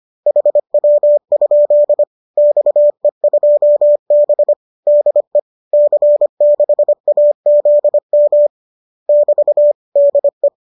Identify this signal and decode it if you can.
Morse code, heard as a keyed tone, about 25 wpm, 590 Hz. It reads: HW? XE2B DE C6AZM <BT> DE
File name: qso204 25